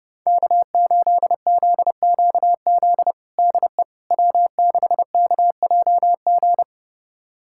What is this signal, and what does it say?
Morse code, heard as a keyed tone, about 32 wpm, 700 Hz. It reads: K8ZQZ DE W6KJG
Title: USA Callsign DE Callsign - 5 Characters - 30wpm